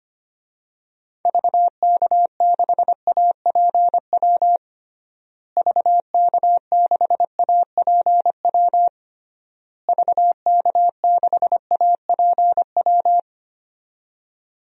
Morse code, heard as a keyed tone, about 25 wpm, 700 Hz. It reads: VK6APW VK6APW VK6APW
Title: DXCC-100 Call Signs - 6 Characters - Repeated 3x - 25wpm